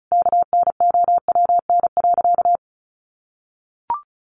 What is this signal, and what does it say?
Morse code, heard as a keyed tone, about 34 wpm, 700 Hz. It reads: KNOWN.
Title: Sentences from Top 1100 Words 35wpm